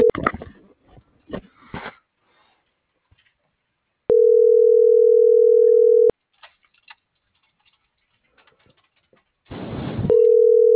Ringback tone (North America), 440 + 480 Hz.